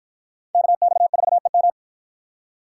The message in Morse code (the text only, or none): KX4EM